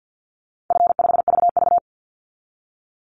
Morse code, heard as a keyed tone, about 50 wpm, 700 Hz. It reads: F5VV
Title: Easy Call Signs Repeated 2x - 50wpm